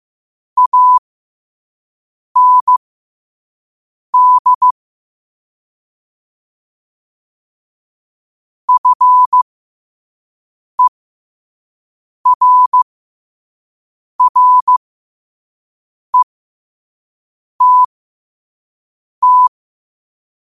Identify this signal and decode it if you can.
Morse code, heard as a keyed tone, about 15 wpm, spaced as for 5 wpm, 1000 Hz. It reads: AND FERRETT
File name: qotdfiles / 5wpm